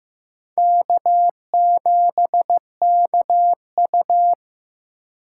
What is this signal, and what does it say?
Morse code, heard as a keyed tone, about 18 wpm, 700 Hz. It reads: K7KU
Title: Speed Racing - USA Call Signs 15wpm